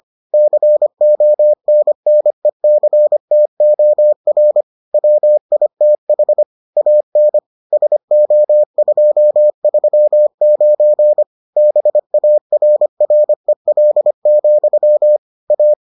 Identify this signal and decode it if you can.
Morse code, heard as a keyed tone, about 25 wpm, 600 Hz. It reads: CONNECTOR WITH AN SO239 BARREL, A